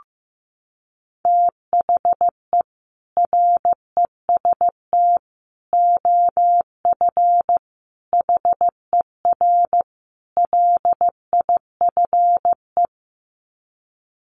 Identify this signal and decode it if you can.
Morse code, heard as a keyed tone, about 15 wpm, 700 Hz. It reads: THE REST OF HER LIFE
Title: Sets of 5 Words Encore 15wpm